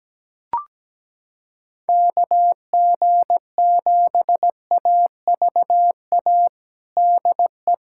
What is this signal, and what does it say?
Morse code, heard as a keyed tone, about 17 wpm, 700 Hz. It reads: KG7AVA DE
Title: USA Callsign DE Callsign - 6 Characters - 17wpm